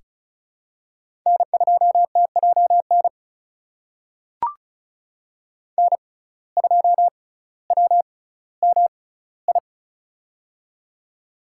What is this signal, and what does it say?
Morse code, heard as a keyed tone, about 36 wpm, spaced as for 18 wpm, 700 Hz. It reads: N2TJN N2WMI
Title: USA Call Signs - 5 Characters - 6x Character Spacing - 35wpm